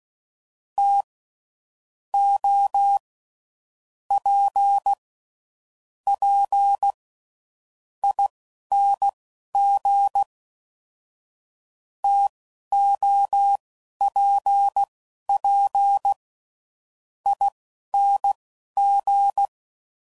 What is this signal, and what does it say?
Morse code, heard as a keyed tone, about 16 wpm, spaced as for 9 wpm, 800 Hz. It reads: TOPPING TOPPING
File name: Mixed / Variable Sp / TOs